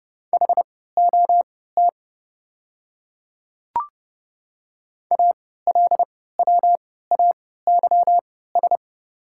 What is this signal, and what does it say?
Morse code, heard as a keyed tone, about 30 wpm, spaced as for 17 wpm, 690 Hz. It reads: HOT ALWAYS
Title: 3x Character Spacing - Top 500 Words - 30wpm